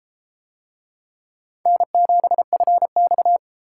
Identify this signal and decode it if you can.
Morse code, heard as a keyed tone, about 30 wpm, 700 Hz. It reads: N7FX